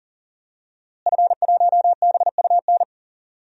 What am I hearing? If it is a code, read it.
Morse code, heard as a keyed tone, about 40 wpm, 700 Hz. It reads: F1BUN
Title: DXCC-100 Call Signs - 5 Characters - Repeated 3x - 40wpm